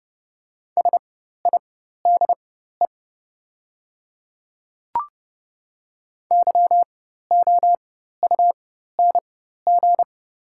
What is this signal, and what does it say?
Morse code, heard as a keyed tone, about 30 wpm, spaced as for 14 wpm, 700 Hz. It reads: SIDE YOUNG
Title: 4x Character Spacing - Top 500 Words - 30wpm